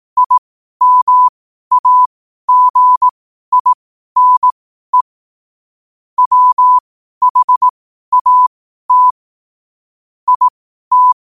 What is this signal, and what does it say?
Morse code, heard as a keyed tone, about 18 wpm, spaced as for 12 wpm, 1000 Hz. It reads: IMAGINE WHAT IT
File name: qotdfiles / 12wpm